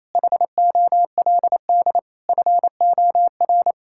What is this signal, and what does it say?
Morse code, heard as a keyed tone, about 27 wpm, 700 Hz. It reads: HOLD FOR
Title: Sentences from Top 1000 Words - Review Part 5 - 28wpm